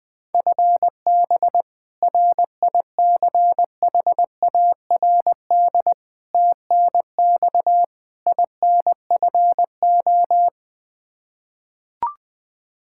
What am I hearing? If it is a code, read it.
Morse code, heard as a keyed tone, about 21 wpm, 700 Hz. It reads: FB RICHARD TNX INFO
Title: Practice QSO - Exchange 3 - More Info - 20wpm